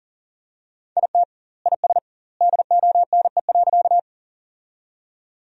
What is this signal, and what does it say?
Morse code, heard as a keyed tone, about 40 wpm, 700 Hz. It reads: IT IS DONE.
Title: 2x Word-Spacing - Sentences from Top 500 Words - 40wpm